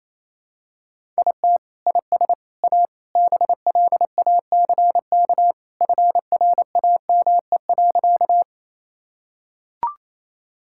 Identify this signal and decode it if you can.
Morse code, heard as a keyed tone, about 28 wpm, 700 Hz. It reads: IT IS A BLACK FRAME.
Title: Sentences from Top 1000 Words - Review Part 6 - 28wpm